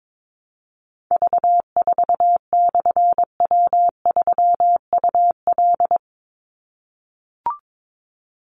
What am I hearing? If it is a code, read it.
Morse code, heard as a keyed tone, about 22 wpm, 700 Hz. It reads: V4/W3UL
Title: Call Signs - 22wpm